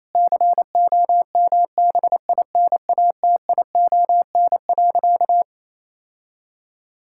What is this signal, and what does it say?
Morse code, heard as a keyed tone, about 27 wpm, 700 Hz. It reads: COMBINATION.
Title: Sentences from Top 1000 Words - Review Part 6 - 28wpm